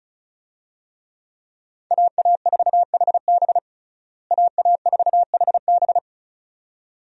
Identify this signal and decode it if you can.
Morse code, heard as a keyed tone, about 34 wpm, 700 Hz. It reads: AA4HB AA4HB
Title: USA Call Signs - 5 Characters - Repeated 2x - 35wpm